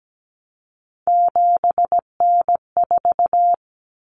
Morse code, heard as a keyed tone, about 17 wpm, 700 Hz. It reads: 7N4
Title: Common Cut Numbers - 3-Digit 17wpm